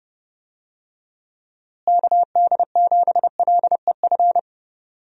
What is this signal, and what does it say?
Morse code, heard as a keyed tone, about 30 wpm, 700 Hz. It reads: KD7LEF
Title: USA Call Signs - 6 Characters - Repeated 2x - 30wpm